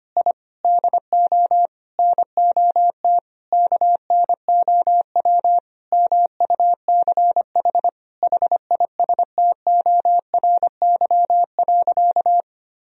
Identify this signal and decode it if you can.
Morse code, heard as a keyed tone, about 25 wpm, 700 Hz. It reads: I DO NOT KNOW MUCH HISTORY.
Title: Sentences from Top 1000 Words - Review Part 2 - 25wpm